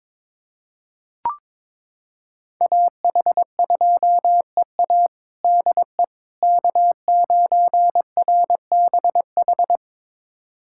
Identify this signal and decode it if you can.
Morse code, heard as a keyed tone, about 22 wpm, 700 Hz. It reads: AH2EA DE K9RBH
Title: USA Callsign DE Callsign - 5 Characters - 22wpm